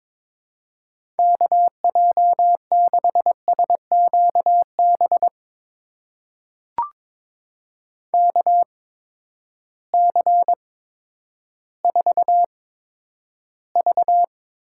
Morse code, heard as a keyed tone, about 22 wpm, spaced as for 11 wpm, 700 Hz. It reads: KJ6SQB KC4V
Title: USA Call Signs - 6 Characters - 8x Character Spacing - 22wpm